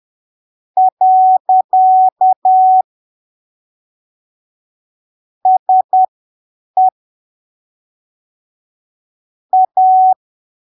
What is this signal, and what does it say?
Morse code, heard as a keyed tone, about 10 wpm, spaced as for 7 wpm, 750 Hz. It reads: . SE A
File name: arquivos / texto / texto 20 10ppm